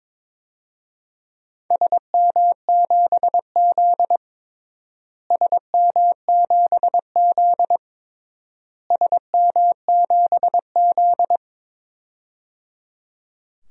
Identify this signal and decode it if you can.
Morse code, heard as a keyed tone, about 22 wpm, 700 Hz. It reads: SM7Z SM7Z SM7Z
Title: DXCC-100 Call Signs - 4 Characters - Repeated 3x - 22wpm